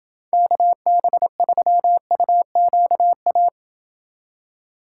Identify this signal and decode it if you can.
Morse code, heard as a keyed tone, about 28 wpm, 700 Hz. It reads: KB3UQA